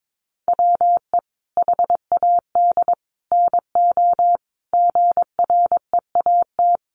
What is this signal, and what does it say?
Morse code, heard as a keyed tone, about 22 wpm, 700 Hz. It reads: WE HAD NO GREAT
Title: Sentences from Top 1200 Words 22wpm